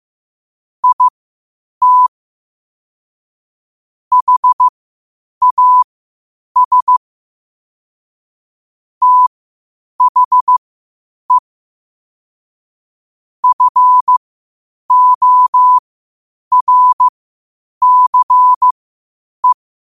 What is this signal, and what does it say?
Morse code, heard as a keyed tone, about 15 wpm, spaced as for 8 wpm, 1000 Hz. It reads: IT HAS THE FORCE
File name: qotdfiles / 8wpm